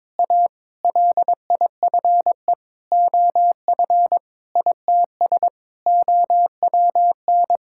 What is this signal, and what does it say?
Morse code, heard as a keyed tone, about 22 wpm, 700 Hz. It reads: A LIFE OF ITS OWN
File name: practice / 22wpm / Sets of 8 Words - 22wpm